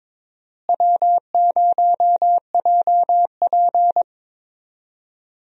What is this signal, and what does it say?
Morse code, heard as a keyed tone, about 22 wpm, 700 Hz. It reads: W0JP
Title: USA Call Signs - 4 Characters - Repeated 2x - 22wpm